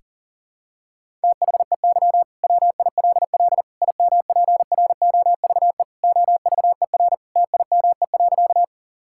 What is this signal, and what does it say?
Morse code, heard as a keyed tone, about 37 wpm, 700 Hz. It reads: THEY WILL IMPROVE OVER TIME.
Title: Sentences from Top 1000 Words - Review Part 5 - 40wpm